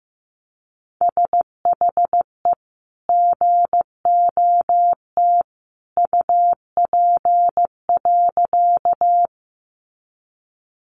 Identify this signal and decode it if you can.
Morse code, heard as a keyed tone, about 15 wpm, 700 Hz. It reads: SHE GOT UP.
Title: Sentences from Top 300 Words 15wpm